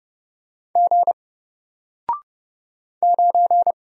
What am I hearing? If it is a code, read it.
Morse code, heard as a keyed tone, about 30 wpm, 700 Hz. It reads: G 9
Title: Single Character (Letter:Number:Punctuation:Prosign) Alt 30wpm